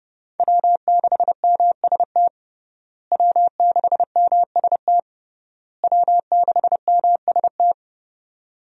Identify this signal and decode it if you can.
Morse code, heard as a keyed tone, about 32 wpm, 700 Hz. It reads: W6MST W6MST W6MST
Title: USA Call Signs - 5 Characters - Repeated 3x - 30wpm